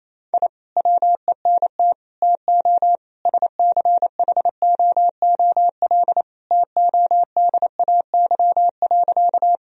Morse code, heard as a keyed tone, about 28 wpm, 700 Hz. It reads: I WENT TO SCHOOL TODAY.
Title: Sentences from Top 1000 Words - Review Part 1 - 28wpm